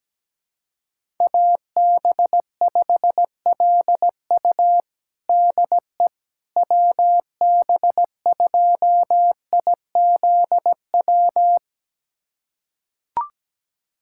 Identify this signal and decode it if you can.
Morse code, heard as a keyed tone, about 17 wpm, 700 Hz. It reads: AB5LU DE WB2IZW
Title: USA Callsign DE Callsign 17wpm